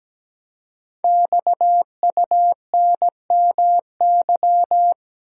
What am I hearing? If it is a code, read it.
Morse code, heard as a keyed tone, about 17 wpm, 700 Hz. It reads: XUNMY
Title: Code Groups - 5 Letters - 17wpm